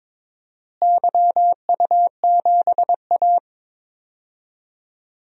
Morse code, heard as a keyed tone, about 22 wpm, 700 Hz. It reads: YU7A
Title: DXCC-100 Call Signs - 4 Characters - Repeated 2x - 22wpm